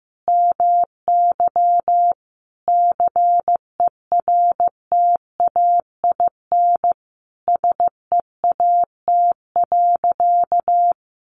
Morse code, heard as a keyed tone, about 15 wpm, 700 Hz. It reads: MY CERTAIN SEAT.
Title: Sentences from Top 1500 Words 15wpm